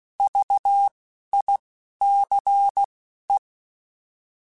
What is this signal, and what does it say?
Morse code, heard as a keyed tone, about 16 wpm, spaced as for 11 wpm, 800 Hz. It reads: VICE
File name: Alphabetical / Alphabetical mp3s / Vs @10